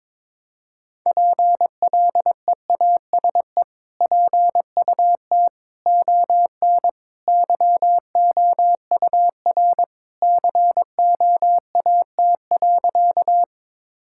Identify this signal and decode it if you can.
Morse code, heard as a keyed tone, about 22 wpm, 700 Hz. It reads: PLEASE PUT ON YOUR COAT.
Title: Sentences from Top 1600 Words 22wpm